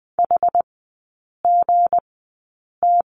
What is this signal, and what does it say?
Morse code, heard as a keyed tone, about 20 wpm, spaced as for 8 wpm, 700 Hz. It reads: HGT
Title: 178 Lesson 29e - Slash - Three Characters 20wpm